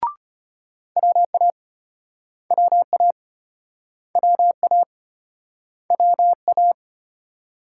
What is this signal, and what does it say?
Morse code, heard as a keyed tone, about 35 wpm, 700 Hz. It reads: WA WA WA WA